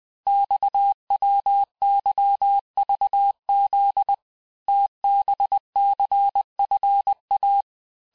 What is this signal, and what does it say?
Morse code, heard as a keyed tone, about 20 wpm, 800 Hz. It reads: XWYVZ TBCFA
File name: cours-858-texte-chiffre-vitesse-19mpm